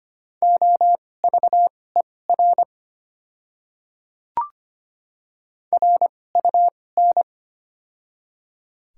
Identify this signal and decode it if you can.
Morse code, heard as a keyed tone, about 25 wpm, spaced as for 18 wpm, 700 Hz. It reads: OVER RUN
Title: 2x Character Spacing - Top 300 Words - 25wpm